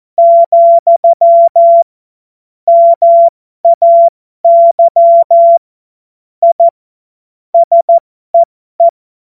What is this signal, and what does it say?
Morse code, heard as a keyed tone, about 14 wpm, spaced as for 12 wpm, 680 Hz. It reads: , MAY I SEE
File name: Morsekurs / Witze